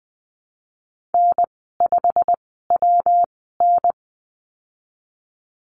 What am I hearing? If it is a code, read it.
Morse code, heard as a keyed tone, about 20 wpm, spaced as for 15 wpm, 700 Hz. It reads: N5WN
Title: 154 Lesson 25g - Letter p - Call Signs 20wpm